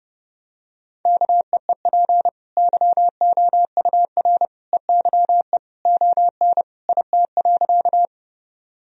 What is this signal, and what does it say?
Morse code, heard as a keyed tone, about 30 wpm, 700 Hz. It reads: KEEP YOUR EYE ON IT.
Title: Sentences from Top 1000 Words - Review Part 1 - 30wpm